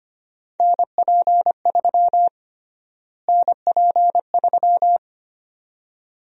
Morse code, heard as a keyed tone, about 25 wpm, 700 Hz. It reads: NP3 NP3
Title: USA Call Sign Prefixes - Repeated 2x - 25wpm